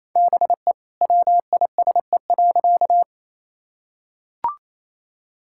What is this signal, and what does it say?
Morse code, heard as a keyed tone, about 27 wpm, 700 Hz. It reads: BE WISE.
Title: Sentences from Top 1600 Words 28wpm